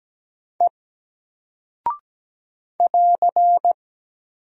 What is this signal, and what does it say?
Morse code, heard as a keyed tone, about 17 wpm, 700 Hz. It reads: E <AR>
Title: Single Character (Letter:Number:Punctuation:Prosign) Alt 17wpm